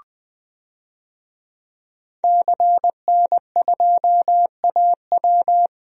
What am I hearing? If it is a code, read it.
Morse code, heard as a keyed tone, about 21 wpm, 700 Hz. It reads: CN2AW
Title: DXCC-100 Call Signs - 5 Characters - Repeated 2x - 20wpm